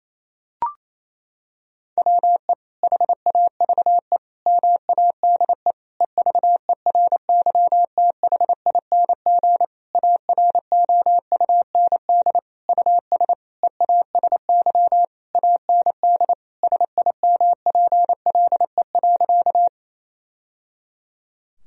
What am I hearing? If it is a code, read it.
Morse code, heard as a keyed tone, about 28 wpm, 700 Hz. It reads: WE HAVE MADE EVERYTHING AROUND US EASY AND SIMPLE.
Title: Sentences from Top 1000 Words - Review Part 3 - 28wpm